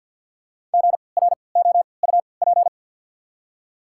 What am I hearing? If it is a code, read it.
Morse code, heard as a keyed tone, about 53 wpm, spaced as for 29 wpm, 690 Hz. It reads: GROUP
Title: 3x Character Spacing - Top 500 Words - 50wpm